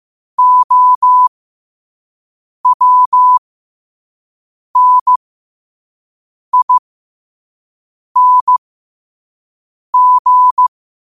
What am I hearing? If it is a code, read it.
Morse code, heard as a keyed tone, about 15 wpm, spaced as for 5 wpm, 1000 Hz. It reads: OWNING
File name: qotdfiles / 5wpm